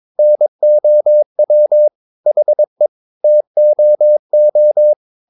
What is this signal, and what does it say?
Morse code, heard as a keyed tone, about 22 wpm, 600 Hz. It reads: NOW HE TOO
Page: Romanen Never af Ken Follet på engelsk.